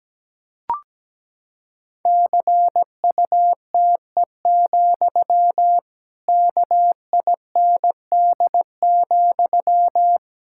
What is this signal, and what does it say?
Morse code, heard as a keyed tone, about 17 wpm, 700 Hz. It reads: CUTE, KIND,
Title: Pangram Challenge v2 - 17wpm